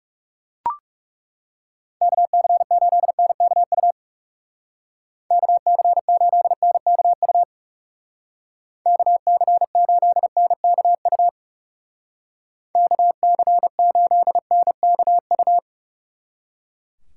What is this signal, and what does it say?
Morse code, heard as a keyed tone, about 38 wpm, 700 Hz. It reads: KC8NKU KC8NKU KC8NKU KC8NKU